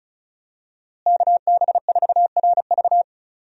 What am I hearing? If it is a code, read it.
Morse code, heard as a keyed tone, about 34 wpm, 700 Hz. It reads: KB4RV
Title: USA Call Signs - 5 Characters - Repeated 3x - 35wpm